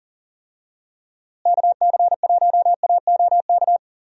Morse code, heard as a keyed tone, about 40 wpm, 700 Hz. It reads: KC1AOK
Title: USA Call Signs - 6 Characters - Repeated 2x - 40wpm